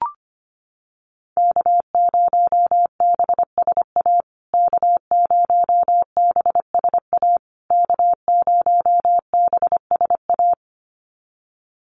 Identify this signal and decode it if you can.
Morse code, heard as a keyed tone, about 25 wpm, 700 Hz. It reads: K0BSA K0BSA K0BSA
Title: USA Call Signs Repeated 3x 25wpm